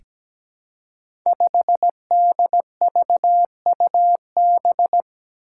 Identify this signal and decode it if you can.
Morse code, heard as a keyed tone, about 17 wpm, 700 Hz. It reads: 5DVUB
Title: Five Challenging Letter-Number Combinations 17wpm